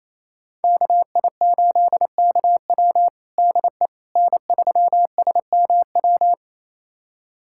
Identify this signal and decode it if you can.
Morse code, heard as a keyed tone, about 28 wpm, 700 Hz. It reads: KI8KW DE N3SMW
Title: USA Callsign DE Callsign - 5 Characters - 28wpm